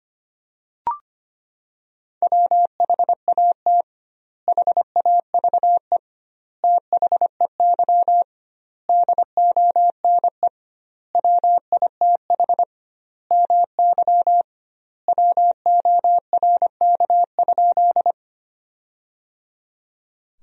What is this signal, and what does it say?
Morse code, heard as a keyed tone, about 25 wpm, 700 Hz. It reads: WHAT HAVE THEY DONE WITH MY WORK?
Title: 2x Word-Spacing - Sentences from Top 500 Words - 25wpm